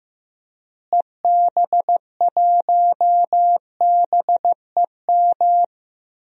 Morse code, heard as a keyed tone, about 16 wpm, 700 Hz. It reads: EB1BEM
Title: DXCC-100 Call Signs - 6 Characters - Repeated 3x - 15wpm